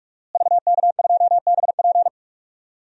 Morse code, heard as a keyed tone, about 42 wpm, 700 Hz. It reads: VK2BP
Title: DXCC-100 Call Signs - 5 Characters - Repeated 2x - 45wpm